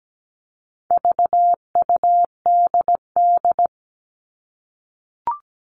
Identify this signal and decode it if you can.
Morse code, heard as a keyed tone, about 17 wpm, 700 Hz. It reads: VUDD
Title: All Cut Numbers - 4-Digit 17wpm